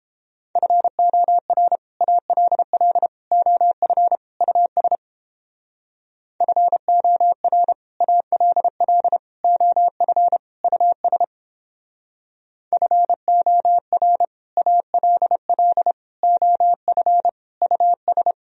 Morse code, heard as a keyed tone, about 29 wpm, 700 Hz. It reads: FOR ALL OF US FOR ALL OF US FOR ALL OF US